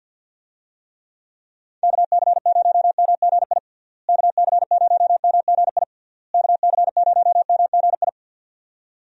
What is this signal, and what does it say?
Morse code, heard as a keyed tone, about 50 wpm, 700 Hz. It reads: KC0MGI KC0MGI KC0MGI
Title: USA Call Signs - 6 Characters - Repeated 3x - 50wpm